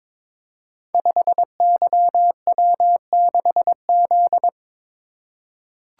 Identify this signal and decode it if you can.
Morse code, heard as a keyed tone, about 22 wpm, 700 Hz. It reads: 5YW6Z
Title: Code Groups - 5 Letter:Numbers - 22wpm